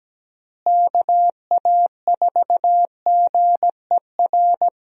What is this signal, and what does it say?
Morse code, heard as a keyed tone, about 17 wpm, 700 Hz. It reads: KA4GER
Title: USA Call Signs - 6 Characters - Repeated 2x - 17wpm